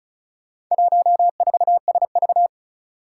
Morse code, heard as a keyed tone, about 36 wpm, 700 Hz. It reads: 14SV
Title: Four Challenging Letter-Number Combinations 35wpm